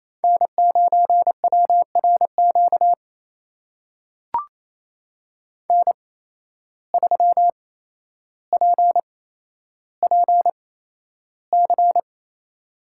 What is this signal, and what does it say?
Morse code, heard as a keyed tone, about 28 wpm, spaced as for 12 wpm, 700 Hz. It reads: N9WRQ N3PPC
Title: USA Call Signs - 5 Characters - 8x Character Spacing - 28wpm